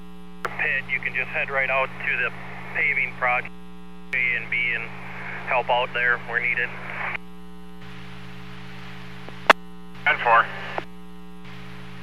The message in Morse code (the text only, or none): none